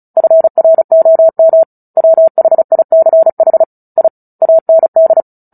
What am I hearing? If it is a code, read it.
Morse code, heard as a keyed tone, about 34 wpm, 650 Hz. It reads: FROM WHICH I AND
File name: Book Two / 09-Chapter 9 - 35 WPM